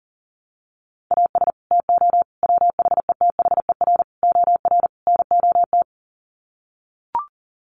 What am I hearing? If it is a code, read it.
Morse code, heard as a keyed tone, about 40 wpm, 700 Hz. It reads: AS TO WHETHER OR NOT
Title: Sets of 5 Words Encore 40wpm